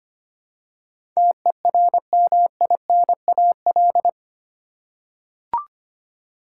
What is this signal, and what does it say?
Morse code, heard as a keyed tone, about 25 wpm, 700 Hz. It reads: TERMINAL
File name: Eight Letter Words - 25wpm